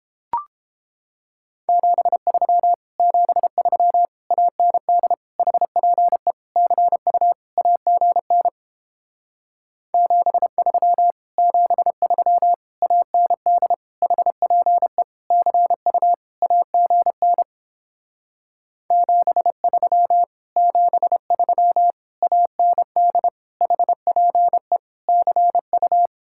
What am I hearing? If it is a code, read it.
Morse code, heard as a keyed tone, about 29 wpm, 700 Hz. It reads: 73 73 AND HPE CU AGN 73 73 AND HPE CU AGN 73 73 AND HPE CU